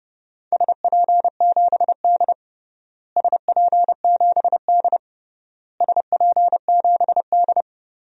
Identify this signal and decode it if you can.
Morse code, heard as a keyed tone, about 32 wpm, 700 Hz. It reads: SP7D SP7D SP7D
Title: DXCC-100 Call Signs - 4 Characters - Repeated 3x - 30wpm